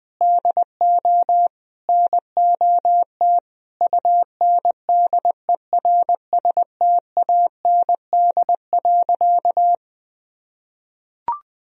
Morse code, heard as a keyed tone, about 20 wpm, 700 Hz. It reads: DO NOT UNDERSTAND.
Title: Sentences from Top 1000 Words - Review Part 1 - 20wpm